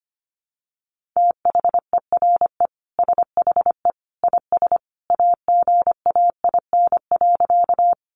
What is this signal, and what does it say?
Morse code, heard as a keyed tone, about 25 wpm, 700 Hz. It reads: THERE SHE IS AGAIN.
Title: Sentences from Top 200 Words 25wpm